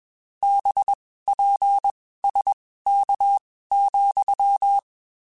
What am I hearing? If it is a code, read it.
Morse code, heard as a keyed tone, about 21 wpm, spaced as for 15 wpm, 800 Hz. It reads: BPSK,
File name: (spaced) / QSOs Spaced-mp3s / kc7sac @20 sp